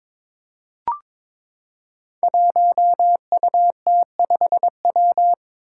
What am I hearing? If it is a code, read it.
Morse code, heard as a keyed tone, about 22 wpm, 700 Hz. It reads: 1UT5W
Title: Five Challenging Letter-Number Combinations 22wpm